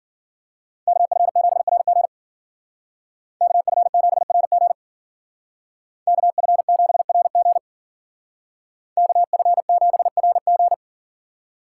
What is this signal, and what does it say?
Morse code, heard as a keyed tone, about 49 wpm, 700 Hz. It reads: KF7RG KF7RG KF7RG KF7RG
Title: Speed Racing - USA 2x2 Call Signs 40wpm